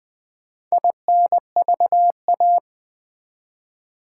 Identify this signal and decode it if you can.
Morse code, heard as a keyed tone, about 20 wpm, 700 Hz. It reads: INVA
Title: Build A Word - 2 Letters At A Time - 8-Letter Words - 20wpm